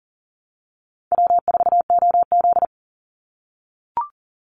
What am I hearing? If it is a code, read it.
Morse code, heard as a keyed tone, about 40 wpm, 700 Hz. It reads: W4OZ
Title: USA Call Signs 40wpm